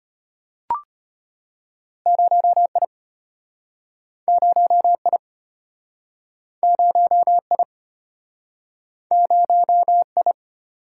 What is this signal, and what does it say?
Morse code, heard as a keyed tone, about 36 wpm, 700 Hz. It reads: 0I 0I 0I 0I